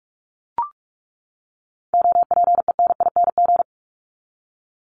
Morse code, heard as a keyed tone, about 44 wpm, 700 Hz. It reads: OPENING
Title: ING Suffix - Top 100 Words with Base Words 45wpm